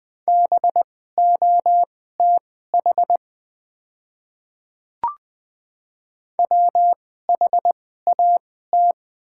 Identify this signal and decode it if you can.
Morse code, heard as a keyed tone, about 20 wpm, spaced as for 14 wpm, 700 Hz. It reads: BOTH WHAT
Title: 2x Character Spacing - Top 500 Words - 20wpm